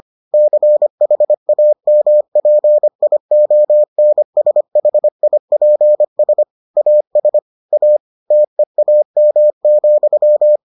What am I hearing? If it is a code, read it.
Morse code, heard as a keyed tone, about 25 wpm, 600 Hz. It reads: CHAMPIONSHIPS AS A TEAM,